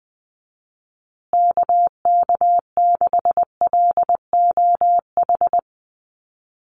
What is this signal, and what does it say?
Morse code, heard as a keyed tone, about 19 wpm, 700 Hz. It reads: KK6LOH
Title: USA Call Signs Repeated 2x 20wpm